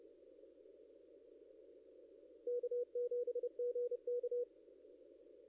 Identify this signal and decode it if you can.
Morse code, heard as a keyed tone, about 27 wpm, 490 Hz. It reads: K7GK
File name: contests / ss-cw-k5tr-op-n5rz -0600